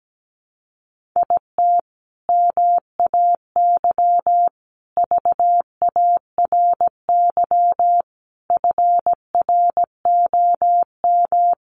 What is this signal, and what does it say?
Morse code, heard as a keyed tone, about 17 wpm, 700 Hz. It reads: IT MAY VARY FROM
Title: Sentences from Top 1200 Words 17wpm